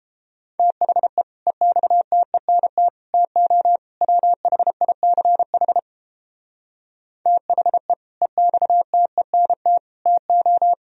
Morse code, heard as a keyed tone, about 31 wpm, 700 Hz. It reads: THE EXTENT TO WHICH THE EXTENT TO